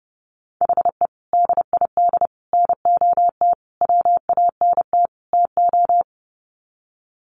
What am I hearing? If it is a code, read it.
Morse code, heard as a keyed tone, about 28 wpm, 700 Hz. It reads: HE DID NOT WANT TO
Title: Sets of 5 Words Encore 30wpm